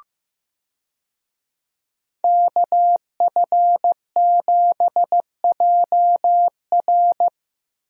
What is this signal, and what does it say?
Morse code, heard as a keyed tone, about 15 wpm, 700 Hz. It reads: KF7JR
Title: USA Call Signs - 5 Characters - 15wpm